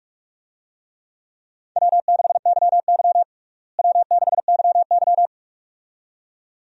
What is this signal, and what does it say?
Morse code, heard as a keyed tone, about 43 wpm, 700 Hz. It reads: W6YY W6YY
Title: USA Call Signs - 4 Characters - Repeated 2x - 45wpm